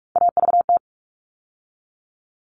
Morse code, heard as a keyed tone, about 44 wpm, 700 Hz. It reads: AVT
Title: All Cut Numbers - 3-Digit 45wpm